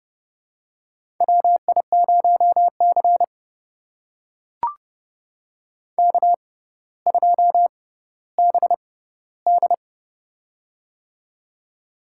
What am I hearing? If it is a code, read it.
Morse code, heard as a keyed tone, about 29 wpm, spaced as for 15 wpm, 700 Hz. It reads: WI0C K2BD
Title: USA Call Signs - 4 Characters - 6x Character Spacing - 30wpm